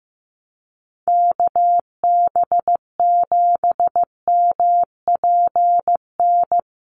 Morse code, heard as a keyed tone, about 15 wpm, 700 Hz. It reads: KB7MPN
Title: USA Call Signs Repeated 3x 15wpm